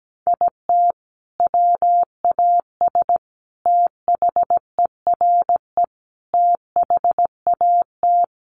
Morse code, heard as a keyed tone, about 17 wpm, 700 Hz. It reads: IT WAS THERE THAT
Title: Sentences from Top 1500 Words 17wpm